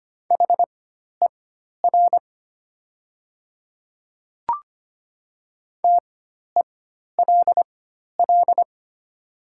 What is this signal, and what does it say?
Morse code, heard as a keyed tone, about 25 wpm, spaced as for 12 wpm, 690 Hz. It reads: HER TELL
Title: 4x Character Spacing - Top 500 Words - 25wpm